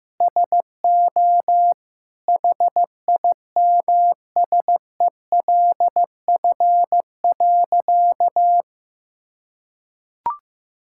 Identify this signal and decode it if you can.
Morse code, heard as a keyed tone, about 15 wpm, 700 Hz. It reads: SO HIMSELF.
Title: Sentences from Top 1600 Words 15wpm